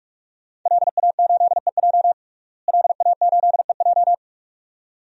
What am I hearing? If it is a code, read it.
Morse code, heard as a keyed tone, about 42 wpm, 700 Hz. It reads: LA8EJ LA8EJ
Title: DXCC-100 Call Signs - 5 Characters - Repeated 2x - 45wpm